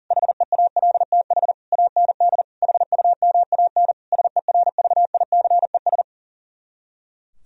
Morse code, heard as a keyed tone, about 51 wpm, 700 Hz. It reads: HEALTH AND HUMAN SERVICES